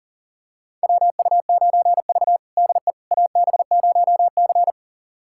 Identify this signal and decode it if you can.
Morse code, heard as a keyed tone, about 40 wpm, 700 Hz. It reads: WU9V DE AB0C
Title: USA Callsign DE Callsign - 4 Characters - 40wpm